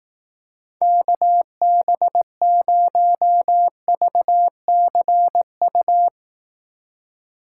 Morse code, heard as a keyed tone, about 18 wpm, 700 Hz. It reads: KB0VCU
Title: Speed Racing - USA 2x3 Call Signs 15wpm